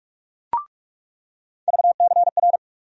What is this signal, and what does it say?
Morse code, heard as a keyed tone, about 47 wpm, 700 Hz. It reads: VCR
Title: USA Call Sign Postfixes - 3 Letters 45wpm